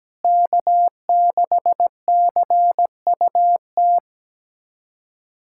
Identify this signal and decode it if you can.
Morse code, heard as a keyed tone, about 17 wpm, 700 Hz. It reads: K6CUT
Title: USA Callsign DE Callsign - 5 Characters - 17wpm